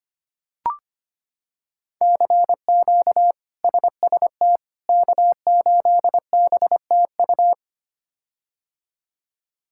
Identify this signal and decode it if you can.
Morse code, heard as a keyed tone, about 25 wpm, 700 Hz. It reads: CQ SST K8BTU
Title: K1USN SST QSO Practice - 25wpm